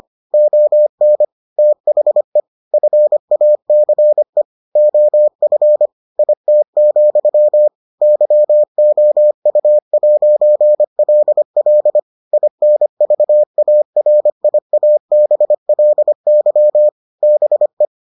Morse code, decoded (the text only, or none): ON THE FACE OF IT, YOU'LL INVARIABLY BE